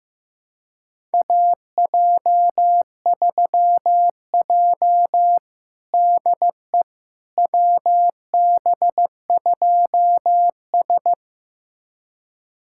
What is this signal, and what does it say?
Morse code, heard as a keyed tone, about 15 wpm, 700 Hz. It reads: AJ3J DE WB2S
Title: USA Callsign DE Callsign - 4 Characters - 15wpm